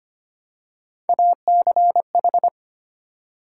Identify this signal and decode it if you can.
Morse code, heard as a keyed tone, about 25 wpm, 700 Hz. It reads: ACH
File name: Three Letter-Number Combinations - 25wpm